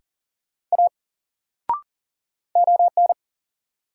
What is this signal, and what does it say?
Morse code, heard as a keyed tone, about 37 wpm, 700 Hz. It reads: A ON
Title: Sentences from Top 100 Words - New Word Drill - 40wpm